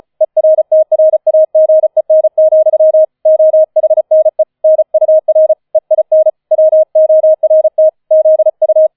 Morse code, heard as a keyed tone, about 34 wpm, 630 Hz. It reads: ERTRAGEN, OHNE NUR EIN WORT ZU